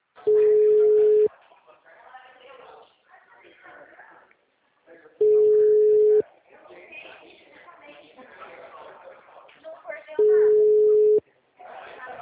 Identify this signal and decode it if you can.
Ringback tone (Europe, 425 Hz), with speech.
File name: out- -2013- - - .26526